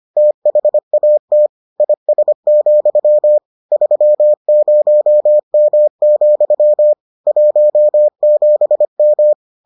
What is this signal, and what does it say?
Morse code, heard as a keyed tone, about 25 wpm, 600 Hz. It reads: THAT IS, 30M, 17M